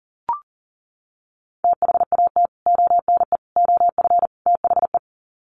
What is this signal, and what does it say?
Morse code, heard as a keyed tone, about 37 wpm, 700 Hz. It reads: THAT ONE OF THE
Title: Sets of 4 Words 40wpm